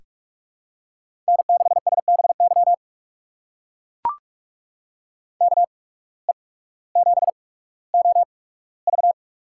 Morse code, heard as a keyed tone, about 44 wpm, spaced as for 19 wpm, 690 Hz. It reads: N6SBY KE7OV
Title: USA Call Signs - 8x Character Spacing - 45wpm